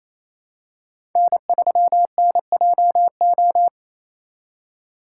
Morse code, heard as a keyed tone, about 27 wpm, 700 Hz. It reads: N3NJO
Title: USA Call Signs - 5 Characters - 4x Character Spacing - 28wpm